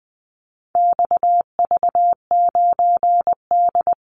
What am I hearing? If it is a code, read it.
Morse code, heard as a keyed tone, about 20 wpm, 700 Hz. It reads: XV9D
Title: Call Signs - 20wpm